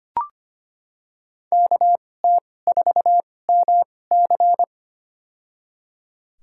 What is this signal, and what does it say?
Morse code, heard as a keyed tone, about 25 wpm, spaced as for 18 wpm, 700 Hz. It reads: KT4MC
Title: USA Call Signs - 5 Characters - 2x Character Spacing - 25wpm